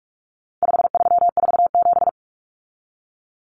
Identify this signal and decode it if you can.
Morse code, heard as a keyed tone, about 44 wpm, 700 Hz. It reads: 5347
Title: Common Cut Numbers - 4-Digit 45wpm